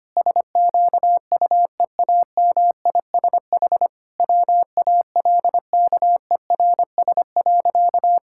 Morse code, heard as a keyed tone, about 25 wpm, 700 Hz. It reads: SQUEAMISH WALKERS.
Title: Pangram Challenge v2 - 25wpm